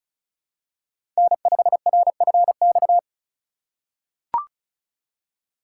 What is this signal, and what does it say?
Morse code, heard as a keyed tone, about 36 wpm, 700 Hz. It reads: N5RFX
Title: USA Call Signs - 4x Character Spacing - 35wpm